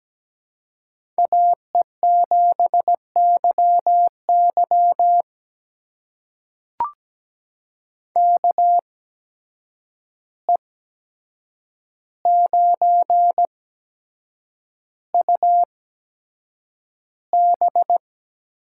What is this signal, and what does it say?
Morse code, heard as a keyed tone, about 17 wpm, spaced as for 7 wpm, 700 Hz. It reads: AE7YY KE9UB
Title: USA Call Signs - 5 Characters - 8x Character Spacing - 17wpm